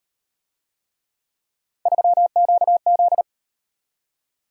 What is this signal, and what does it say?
Morse code, heard as a keyed tone, about 38 wpm, 700 Hz. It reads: 3QZ